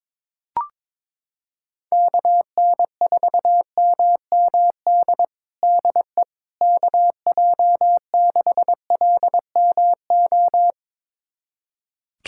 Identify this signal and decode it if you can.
Morse code, heard as a keyed tone, about 22 wpm, 700 Hz. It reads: KN4MMD DE KJ6LMO
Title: USA Callsign DE Callsign - 6 Characters - 22wpm